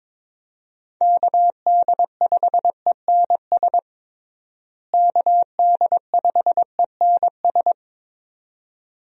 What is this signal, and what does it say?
Morse code, heard as a keyed tone, about 22 wpm, 700 Hz. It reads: KD5ENS KD5ENS
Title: USA Call Signs - 6 Characters - Repeated 2x - 22wpm